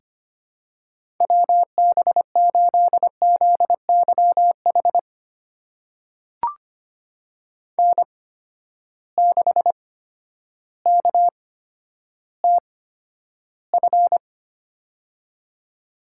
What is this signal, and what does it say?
Morse code, heard as a keyed tone, about 25 wpm, spaced as for 11 wpm, 700 Hz. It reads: WB8ZYH N6KTF
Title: USA Call Signs - 8x Character Spacing - 25wpm